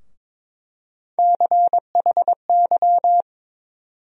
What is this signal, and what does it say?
Morse code, heard as a keyed tone, about 22 wpm, 700 Hz. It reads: CHY
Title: USA Call Sign Postfixes - 3 Letters 22wpm